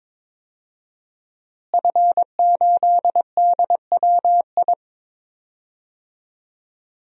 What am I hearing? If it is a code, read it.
Morse code, heard as a keyed tone, about 22 wpm, 700 Hz. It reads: F8DWI
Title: DXCC-100 Call Signs - 5 Characters - Repeated 3x - 22wpm